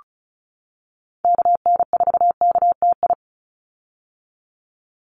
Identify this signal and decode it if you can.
Morse code, heard as a keyed tone, about 34 wpm, 700 Hz. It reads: KN4KTI
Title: USA Call Signs 35wpm